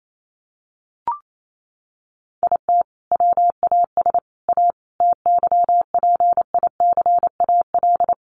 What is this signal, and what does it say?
Morse code, heard as a keyed tone, about 28 wpm, 700 Hz. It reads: IT WAS A TYPICAL
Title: Sentences from Top 1100 Words 28wpm